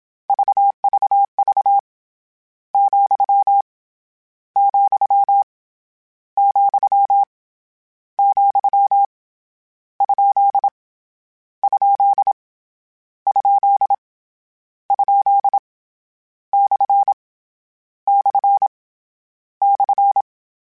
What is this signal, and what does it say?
Morse code, heard as a keyed tone, about 27 wpm, 800 Hz. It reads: VVV , , , , ? ? ? ? / / /
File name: Misc / Misc-mp3s / Punct @25